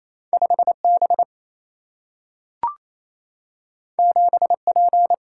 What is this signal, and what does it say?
Morse code, heard as a keyed tone, about 28 wpm, 700 Hz. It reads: 5 B 7 P
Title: Single-Letter Single-Number 28wpm